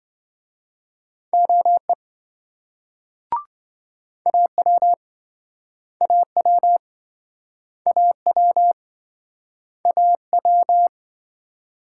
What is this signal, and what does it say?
Morse code, heard as a keyed tone, about 28 wpm, 700 Hz. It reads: OE AW AW AW AW